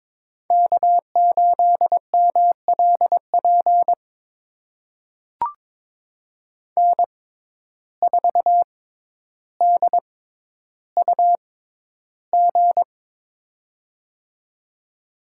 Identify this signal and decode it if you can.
Morse code, heard as a keyed tone, about 22 wpm, spaced as for 11 wpm, 700 Hz. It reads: K8MLP N4DUG
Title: USA Call Signs - 5 Characters - 6x Character Spacing - 22wpm